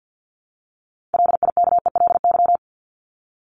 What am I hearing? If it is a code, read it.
Morse code, heard as a keyed tone, about 50 wpm, 700 Hz. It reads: LIKELY
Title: LY Suffix - Top 100 Words with Base Words 50wpm